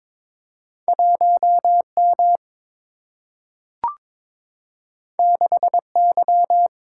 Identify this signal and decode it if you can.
Morse code, heard as a keyed tone, about 22 wpm, 700 Hz. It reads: 1 M 6 Y
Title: Single-Letter Single-Number 22wpm